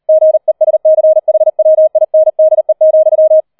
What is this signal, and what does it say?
Morse code, heard as a keyed tone, about 38 wpm, 630 Hz. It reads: GESCHWINDE,
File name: 004 Vierter Streich 40wpm